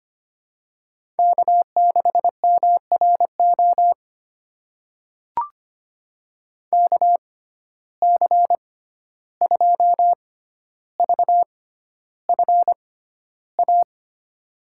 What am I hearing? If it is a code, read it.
Morse code, heard as a keyed tone, about 26 wpm, spaced as for 12 wpm, 700 Hz. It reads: K6MRO KC2VFA
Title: USA Call Signs - 6x Character Spacing - 25wpm